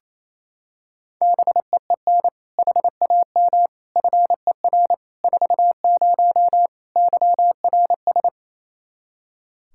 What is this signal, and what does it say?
Morse code, heard as a keyed tone, about 28 wpm, 700 Hz. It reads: BEEN HAM FER 40 YRS
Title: Practice QSO - Exchange 3 - More Info - 28wpm